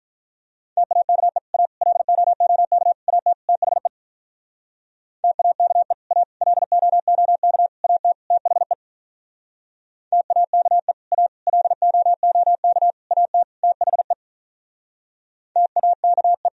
Morse code, heard as a keyed tone, about 46 wpm, 700 Hz. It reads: TAKE A LOOK AT THE TAKE A LOOK AT THE TAKE A LOOK AT THE TAKE